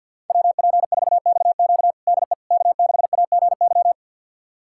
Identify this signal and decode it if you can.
Morse code, heard as a keyed tone, about 52 wpm, 700 Hz. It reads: WP4XQ DE K6AGY
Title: USA Callsign DE Callsign 50wpm